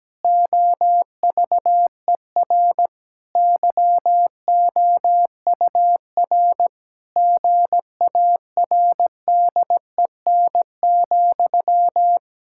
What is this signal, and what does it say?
Morse code, heard as a keyed tone, about 17 wpm, 700 Hz. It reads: OVER YOUR GARDEN,
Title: Pangram Challenge v2 - 17wpm